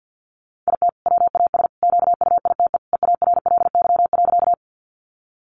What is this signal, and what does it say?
Morse code, heard as a keyed tone, about 50 wpm, 700 Hz. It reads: IT WAS QUITE EARLY.
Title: Sentences from Top 700 Words 50wpm